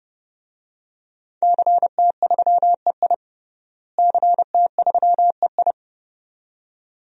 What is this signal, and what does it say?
Morse code, heard as a keyed tone, about 32 wpm, 700 Hz. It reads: CT3EI CT3EI
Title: DXCC-100 Call Signs - 5 Characters - Repeated 2x - 30wpm